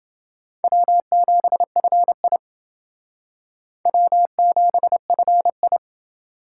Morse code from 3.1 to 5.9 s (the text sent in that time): W7FI